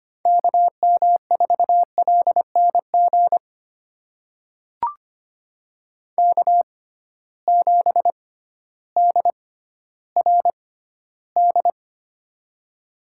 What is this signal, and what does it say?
Morse code, heard as a keyed tone, about 25 wpm, spaced as for 14 wpm, 700 Hz. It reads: KM4LNG K7DRD
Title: USA Call Signs - 6x Character Spacing - 25wpm